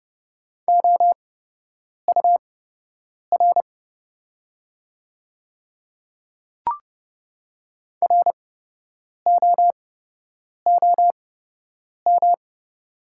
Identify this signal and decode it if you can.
Morse code, heard as a keyed tone, about 30 wpm, spaced as for 8 wpm, 700 Hz. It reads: OUR ROOM
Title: 8x Character Spacing - Top 300 Words - 30wpm